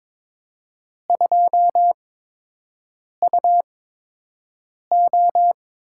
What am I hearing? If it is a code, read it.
Morse code, heard as a keyed tone, about 22 wpm, spaced as for 6 wpm, 700 Hz. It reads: 2UO
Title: USA Call Signs - 4 Characters - 8x Character Spacing - 22wpm